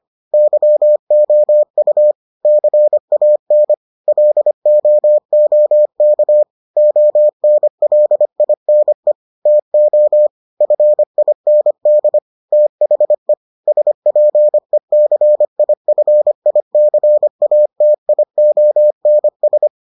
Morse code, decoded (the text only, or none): YOU CAN LOOK ONLINE TO FIND THE SPECIFICATIONS